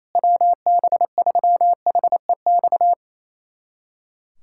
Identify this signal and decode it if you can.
Morse code, heard as a keyed tone, about 28 wpm, 700 Hz. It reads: WB3HEX
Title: USA Callsign DE Callsign - 6 Characters - 28wpm